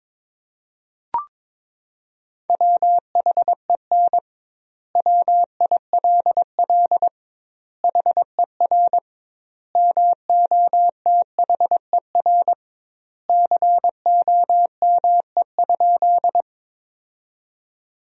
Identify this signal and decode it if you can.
Morse code, heard as a keyed tone, about 22 wpm, 700 Hz. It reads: WHEN WILL HER MOTHER COME?
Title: 2x Word-Spacing - Sentences from Top 200 Words - 22wpm